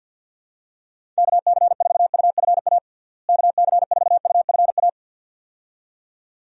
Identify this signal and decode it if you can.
Morse code, heard as a keyed tone, about 51 wpm, 700 Hz. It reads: KC4UFA KC4UFA
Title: USA Call Signs - 6 Characters - Repeated 2x - 50wpm